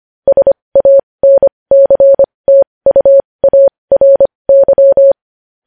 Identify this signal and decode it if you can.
Morse code, heard as a keyed tone, about 25 wpm, spaced as for 20 wpm, 550 Hz. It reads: SANCTUARY
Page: news headlines in morse code at 20 wpm